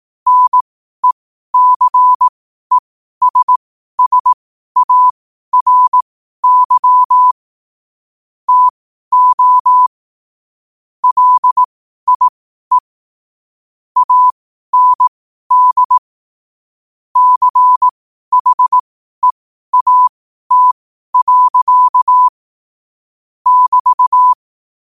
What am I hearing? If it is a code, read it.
Morse code, heard as a keyed tone, about 18 wpm, spaced as for 12 wpm, 1000 Hz. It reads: NECESSARY TO LIE AND CHEAT. <BT>
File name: qotdfiles / 12wpm